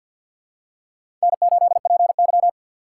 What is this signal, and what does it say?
Morse code, heard as a keyed tone, about 51 wpm, 700 Hz. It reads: N8PY
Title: USA Call Signs - 4 Characters - 6x Character Spacing - 50wpm